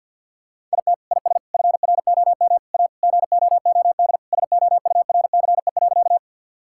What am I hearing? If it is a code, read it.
Morse code, heard as a keyed tone, about 50 wpm, 700 Hz. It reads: IT IS FROM A GOOD SOURCE.
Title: Sentences from Top 1000 Words - Review Part 5 - 50wpm